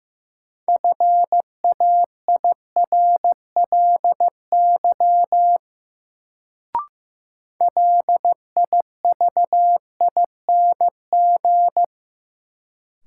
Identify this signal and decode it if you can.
Morse code, heard as a keyed tone, about 15 wpm, 700 Hz. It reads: FAIRLY LIVING
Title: Sentences from Top 1100 Words - New Word Drill Alt - 15wpm